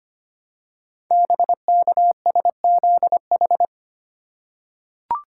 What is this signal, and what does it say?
Morse code, heard as a keyed tone, about 25 wpm, 700 Hz. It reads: BKSZH
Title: Code Groups - 5 Letters - 25wpm